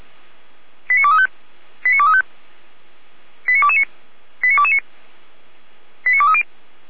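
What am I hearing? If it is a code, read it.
Five-tone selective call (ZVEI), heard as quick runs of five tones, tones of about 70 ms. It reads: 89235 89235 89229 89229 89230
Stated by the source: ZVEI five-tone call received on SDR.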